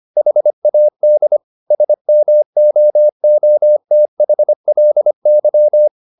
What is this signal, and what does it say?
Morse code, heard as a keyed tone, about 25 wpm, 600 Hz. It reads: HAD SMOOTHLY